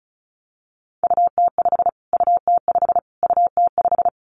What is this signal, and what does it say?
Morse code, heard as a keyed tone, about 34 wpm, 700 Hz. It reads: UT5 UT5 UT5
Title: Callsign Prefixes for Top 25 DXCC - Repeated 3x - 35wpm